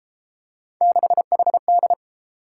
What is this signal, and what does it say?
Morse code, heard as a keyed tone, about 30 wpm, 700 Hz. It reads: 6HD